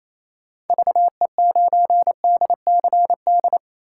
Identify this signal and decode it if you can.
Morse code, heard as a keyed tone, about 28 wpm, 700 Hz. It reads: VE9DCD
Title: K1USN SST QSO Practice - 28wpm